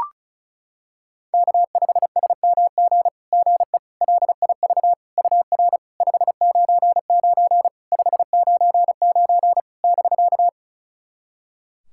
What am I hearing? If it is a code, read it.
Morse code, heard as a keyed tone, about 36 wpm, 690 Hz. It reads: K5SMG GE LIV UR 599 599 <BK>
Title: SOTA QSO Practice 35wpm